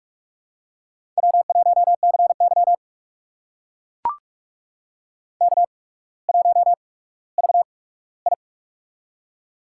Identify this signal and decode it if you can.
Morse code, heard as a keyed tone, about 44 wpm, spaced as for 19 wpm, 690 Hz. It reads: W1CY K1VI
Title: USA Call Signs - 4 Characters - 8x Character Spacing - 45wpm